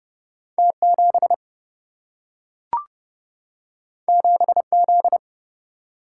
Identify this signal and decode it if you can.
Morse code, heard as a keyed tone, about 31 wpm, 700 Hz. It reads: T7 7Z
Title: Single-Letter Single-Number 30wpm